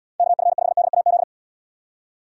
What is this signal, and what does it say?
Morse code, heard as a keyed tone, about 74 wpm, 700 Hz. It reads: KF5DAZ